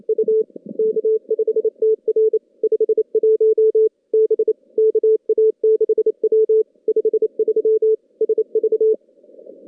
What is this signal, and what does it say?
Morse code, heard as a keyed tone, about 28 wpm, 440 Hz. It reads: U K5TR 51 B KA6W 53 SV
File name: contests / CW-SS-k5tr- -2200